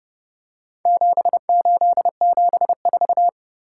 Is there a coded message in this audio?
Yes: Morse code from 0.9 to 3.3 s, heard as a keyed tone, about 30 wpm, 700 Hz. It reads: 7874